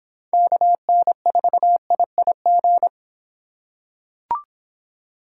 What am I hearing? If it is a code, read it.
Morse code, heard as a keyed tone, about 24 wpm, 700 Hz. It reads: KN4IIG